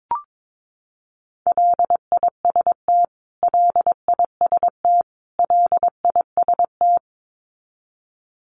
Morse code, heard as a keyed tone, about 22 wpm, 700 Hz. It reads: LIST LIST LIST
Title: Top 500 Words Repeated 22wpm